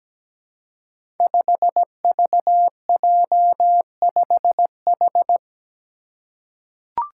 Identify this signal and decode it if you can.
Morse code, heard as a keyed tone, about 17 wpm, 700 Hz. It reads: 5VJ5H
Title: Five Challenging Letter-Number Combinations 17wpm